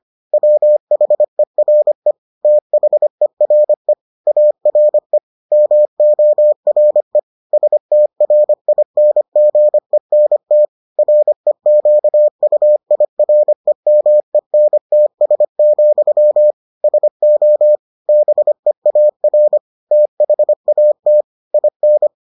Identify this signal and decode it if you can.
Morse code, heard as a keyed tone, about 25 wpm, 600 Hz. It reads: WHERE THERE ARE MORE STRINGENT REQUIREMENTS, SO BEAR THAT IN